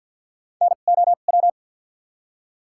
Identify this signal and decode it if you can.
Morse code, heard as a keyed tone, about 53 wpm, spaced as for 37 wpm, 700 Hz. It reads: NOW
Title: 2x Character Spacing - Top 100 Words - 50wpm